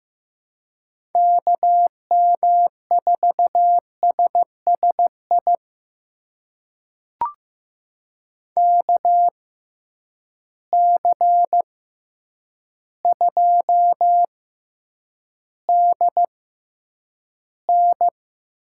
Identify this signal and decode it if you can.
Morse code, heard as a keyed tone, about 15 wpm, spaced as for 8 wpm, 700 Hz. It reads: KM4SSI KC2DN
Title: USA Call Signs - 6 Characters - 6x Character Spacing - 15wpm